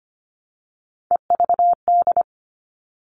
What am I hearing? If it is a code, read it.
Morse code, heard as a keyed tone, about 25 wpm, 700 Hz. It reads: EVD
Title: All Cut Numbers - 3-Digit 25wpm